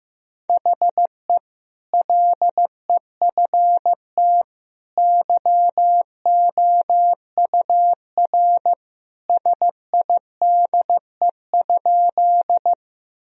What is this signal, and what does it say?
Morse code, heard as a keyed tone, about 15 wpm, 700 Hz. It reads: HE LEFT YOUR SIDE?
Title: Sentences from Top 1000 Words - Review Part 1 - 15wpm